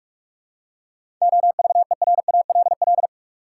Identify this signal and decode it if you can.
Morse code, heard as a keyed tone, about 44 wpm, 700 Hz. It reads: OVERALL
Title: Seven Letter Words - 45wpm